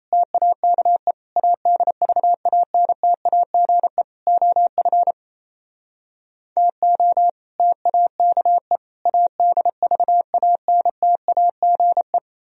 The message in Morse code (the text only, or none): TAKE ADVANTAGE OF TO TAKE ADVANTAGE